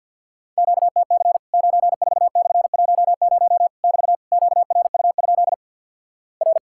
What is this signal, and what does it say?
Morse code, heard as a keyed tone, about 49 wpm, 700 Hz. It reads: QTC 94/10 <BT> QRU? R
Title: Worked All Europe DX Contest - QTC Exchange - 50wpm